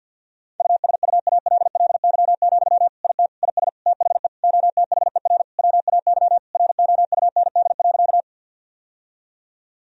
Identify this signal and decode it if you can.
Morse code, heard as a keyed tone, about 50 wpm, 700 Hz. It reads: USUALLY, IT IS THE OTHER WAY ROUND.
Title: Sentences from Top 1000 Words - Review Part 3 - 50wpm